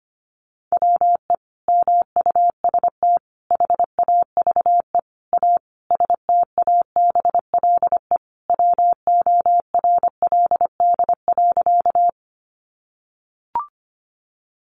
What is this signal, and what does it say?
Morse code, heard as a keyed tone, about 25 wpm, 700 Hz. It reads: WE MUST HAVE A STABLE WORLD.
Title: Sentences from Top 1500 Words 25wpm